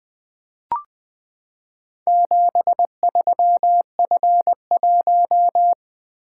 Morse code, heard as a keyed tone, about 21 wpm, 700 Hz. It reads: 73F1
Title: Four Letter-Number Combinations - 20wpm